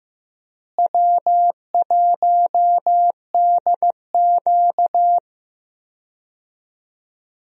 Morse code, heard as a keyed tone, about 15 wpm, 700 Hz. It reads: W1DQ
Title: USA Call Signs - 4 Characters - Repeated 2x - 15wpm